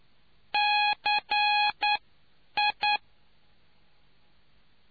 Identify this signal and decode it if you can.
Morse code, heard as a keyed tone, about 10 wpm, spaced as for 8 wpm, 830 Hz. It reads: CI